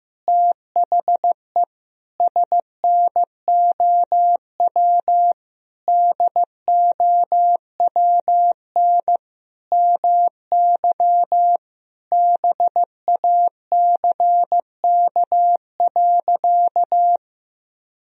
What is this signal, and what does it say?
Morse code, heard as a keyed tone, about 15 wpm, 700 Hz. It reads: THE SNOW DOWN MY BACK.
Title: Sentences from Top 1000 Words - Review Part 3 - 15wpm